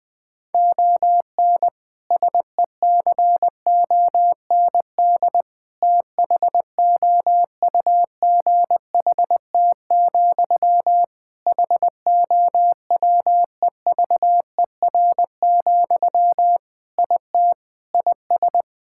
Morse code, heard as a keyed tone, about 20 wpm, 700 Hz. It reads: ON SECOND THOUGHT, HOWEVER, IT IS
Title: Sentences from Top 1600 Words 20wpm